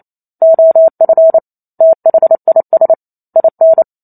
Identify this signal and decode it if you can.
Morse code, heard as a keyed tone, about 28 wpm, 650 Hz. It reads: OF THIS IN
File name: Chapter 2 - 28 WPM